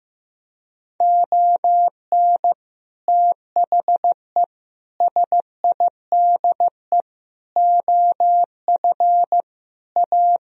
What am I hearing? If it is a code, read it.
Morse code, heard as a keyed tone, about 15 wpm, 700 Hz. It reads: ON THE SIDE OF A
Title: Sentences from Top 1000 Words - Review Part 3 - 15wpm